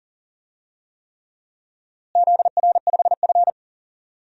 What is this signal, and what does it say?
Morse code, heard as a keyed tone, about 38 wpm, 700 Hz. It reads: ZR5F
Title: DXCC-100 Call Signs - 4 Characters - Repeated 3x - 40wpm